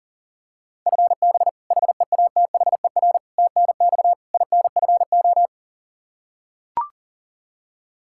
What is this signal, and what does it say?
Morse code, heard as a keyed tone, about 40 wpm, 700 Hz. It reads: FB HEATHER TNX INFO
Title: Practice QSO - Exchange 3 - More Info - 40wpm